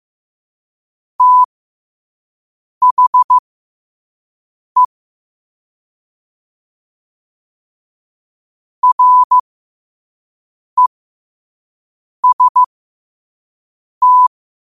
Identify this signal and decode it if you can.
Morse code, heard as a keyed tone, about 15 wpm, spaced as for 5 wpm, 1000 Hz. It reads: THE REST
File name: qotdfiles / 5wpm